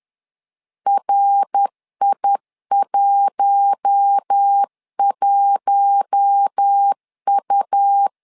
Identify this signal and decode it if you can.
Morse code, heard as a keyed tone, about 10 wpm, 800 Hz. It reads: RI11U
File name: LEZIONI / 4 Q1 10wpm 2448